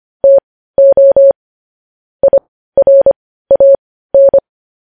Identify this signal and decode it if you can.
Morse code, heard as a keyed tone, about 25 wpm, spaced as for 15 wpm, 550 Hz. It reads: TO IRAN